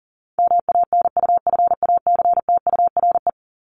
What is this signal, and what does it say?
Morse code, heard as a keyed tone, about 40 wpm, 700 Hz. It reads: MANUFACTURE
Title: ING Suffix - Top 100 Words with Base Words 40wpm